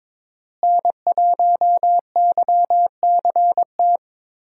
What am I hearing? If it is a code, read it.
Morse code, heard as a keyed tone, about 22 wpm, 700 Hz. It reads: N1YCT
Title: USA Call Signs - 5 Characters - Repeated 3x - 22wpm